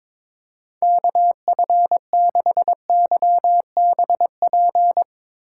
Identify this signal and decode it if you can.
Morse code, heard as a keyed tone, about 21 wpm, 700 Hz. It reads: KF6YBP
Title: USA Call Signs - 3x Character Spacing - 22wpm